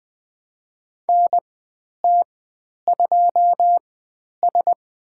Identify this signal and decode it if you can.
Morse code, heard as a keyed tone, about 20 wpm, spaced as for 10 wpm, 700 Hz. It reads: NT2S
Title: 109 Lesson 18g - Letter m - Call Signs 20wpm